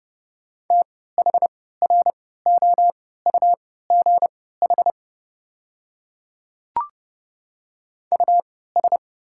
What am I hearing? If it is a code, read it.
Morse code, heard as a keyed tone, about 30 wpm, spaced as for 17 wpm, 700 Hz. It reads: THROUGH US
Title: 3x Character Spacing - Top 300 Words - 30wpm